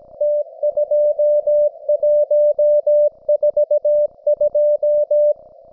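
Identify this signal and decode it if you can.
Morse code, heard as a keyed tone, about 18 wpm, 590 Hz. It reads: T2142
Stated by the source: Russian military Morse station, M32, RDL, on 5268 kHz.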